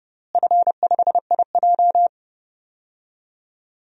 Morse code, heard as a keyed tone, about 30 wpm, 700 Hz. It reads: F5IJ
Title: DXCC-100 Call Signs - 4 Characters - Repeated 2x - 30wpm